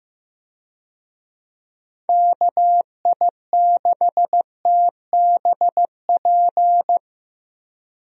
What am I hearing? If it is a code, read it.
Morse code, heard as a keyed tone, about 15 wpm, 700 Hz. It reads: KI6TBP
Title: USA Call Signs - 6 Characters - Repeated 2x - 15wpm